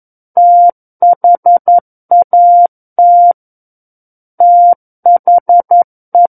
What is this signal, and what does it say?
Morse code, heard as a keyed tone, about 11 wpm, 700 Hz. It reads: THAT THE
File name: 11 WPM